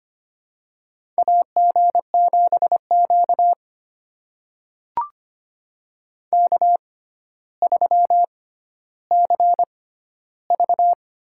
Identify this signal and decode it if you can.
Morse code, heard as a keyed tone, about 25 wpm, spaced as for 13 wpm, 700 Hz. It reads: AG7Q K3CV
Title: USA Call Signs - 4 Characters - 6x Character Spacing - 25wpm